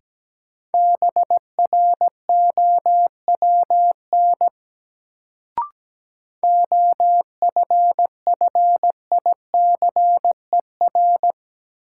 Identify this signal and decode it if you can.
Morse code, heard as a keyed tone, about 17 wpm, 700 Hz. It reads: BROWN OFFICER
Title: Sentences from Top 1200 Words - New Word Drill Alt - 17wpm